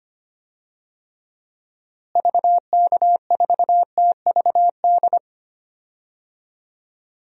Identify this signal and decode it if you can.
Morse code, heard as a keyed tone, about 25 wpm, 700 Hz. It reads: VK4TVD
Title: DXCC-100 Call Signs - 6 Characters - Repeated 2x - 25wpm